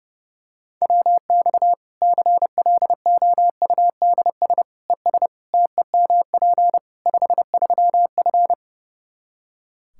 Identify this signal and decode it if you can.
Morse code, heard as a keyed tone, about 30 wpm, 700 Hz. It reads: WX CLOUDS ES TEMP 53F
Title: Practice QSO - Exchange 2 - Info - 30wpm